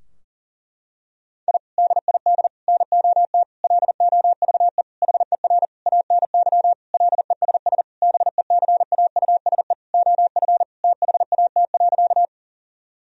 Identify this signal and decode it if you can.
Morse code, heard as a keyed tone, about 40 wpm, 700 Hz. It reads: I DID NOT LOVE HER ANY LESS BECAUSE OF THAT.
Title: Sentences from Top 1000 Words - Review Part 2 - 40wpm